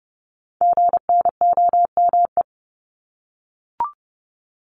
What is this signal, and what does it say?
Morse code, heard as a keyed tone, about 30 wpm, 700 Hz. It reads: GNOME
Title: Multi-letter Phonemes 30wpm